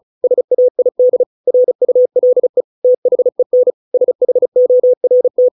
Morse code, heard as a keyed tone, about 34 wpm, 500 Hz. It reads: SAID RULE THEN SHORT